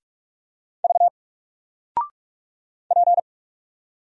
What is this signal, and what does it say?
Morse code, heard as a keyed tone, about 44 wpm, 700 Hz. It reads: V P
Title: Single Letter-Number 45wpm